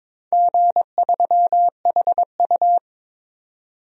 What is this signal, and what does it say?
Morse code, heard as a keyed tone, about 22 wpm, 700 Hz. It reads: G3HU
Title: DXCC-100 Call Signs - 4 Characters - 22wpm